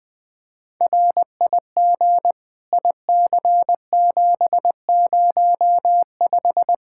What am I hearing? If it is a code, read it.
Morse code, heard as a keyed tone, about 21 wpm, 700 Hz. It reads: RIG IC705
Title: Practice QSO - Exchange 2 - Info - 20wpm